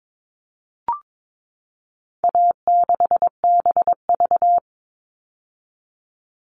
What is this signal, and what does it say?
Morse code, heard as a keyed tone, about 22 wpm, 700 Hz. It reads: A6BV
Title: All Cut Numbers - 4-Digit 22wpm